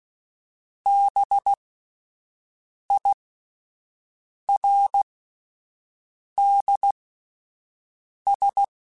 Morse code, heard as a keyed tone, about 16 wpm, spaced as for 5 wpm, 800 Hz. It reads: BIRDS